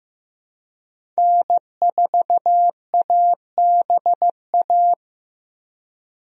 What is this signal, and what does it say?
Morse code, heard as a keyed tone, about 15 wpm, 700 Hz. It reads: N4ABA
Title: USA Call Signs - 5 Characters - 15wpm